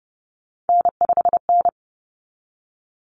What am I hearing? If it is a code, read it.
Morse code, heard as a keyed tone, about 30 wpm, 700 Hz. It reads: N5N
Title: Common Cut Numbers - 3-Digit 30wpm